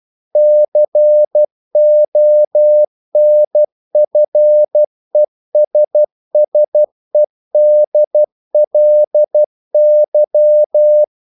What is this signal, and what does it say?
Morse code, heard as a keyed tone, about 12 wpm, 600 Hz. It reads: CONFESSEDLY
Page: English quotations in CW (telegraphy - morse)